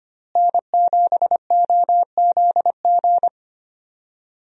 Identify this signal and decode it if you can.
Morse code, heard as a keyed tone, about 25 wpm, 700 Hz. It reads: N7OZG
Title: USA Call Signs - 5 Characters - Repeated 3x - 25wpm